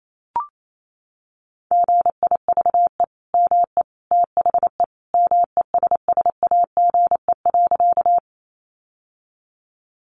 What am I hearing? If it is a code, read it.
Morse code, heard as a keyed tone, about 28 wpm, 700 Hz. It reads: GIVE ME THE MESSAGE.
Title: Sentences from Top 1100 Words 28wpm